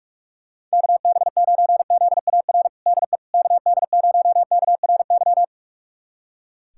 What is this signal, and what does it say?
Morse code, heard as a keyed tone, about 43 wpm, 700 Hz. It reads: KB9ZAR DE KD0KRY
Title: USA Callsign DE Callsign - 6 Characters - 45wpm